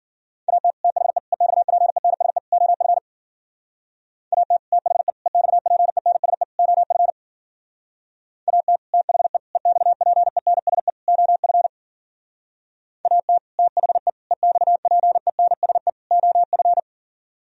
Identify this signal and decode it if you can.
Morse code, heard as a keyed tone, about 50 wpm, 700 Hz. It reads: AT THE EXPENSE OF AT THE EXPENSE OF AT THE EXPENSE OF AT THE EXPENSE OF
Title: Speed-Racing - Sets of 4 Words 40wpm